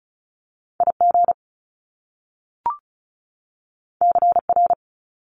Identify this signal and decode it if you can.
Morse code, heard as a keyed tone, about 36 wpm, 700 Hz. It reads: IG CR
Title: Top Two Letter Combinations in Words 35wpm